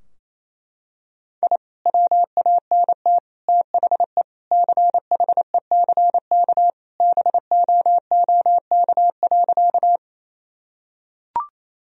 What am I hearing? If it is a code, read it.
Morse code, heard as a keyed tone, about 27 wpm, 700 Hz. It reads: I WANT THE CHECK BOOK.
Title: Sentences from Top 1000 Words - Review Part 3 - 28wpm